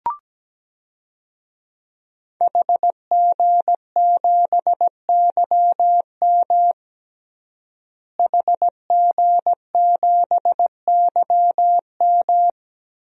Morse code, heard as a keyed tone, about 17 wpm, 700 Hz. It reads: HG7YM HG7YM
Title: DXCC-100 Call Signs - 5 Characters - Repeated 2x - 17wpm